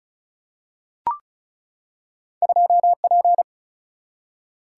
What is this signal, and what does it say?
Morse code, heard as a keyed tone, about 35 wpm, 700 Hz. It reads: 2P
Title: Single-Letter Single-Number 35wpm